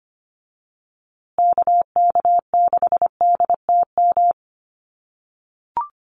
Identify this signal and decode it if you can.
Morse code, heard as a keyed tone, about 25 wpm, 700 Hz. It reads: KK6DTM
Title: USA Call Signs Repeated 3x 25wpm